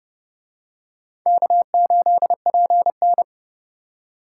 Morse code, heard as a keyed tone, about 28 wpm, 700 Hz. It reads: K8PN